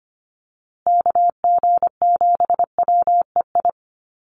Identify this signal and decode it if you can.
Morse code, heard as a keyed tone, about 25 wpm, 700 Hz. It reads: KG7WEI
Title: USA Call Signs Repeated 2x 25wpm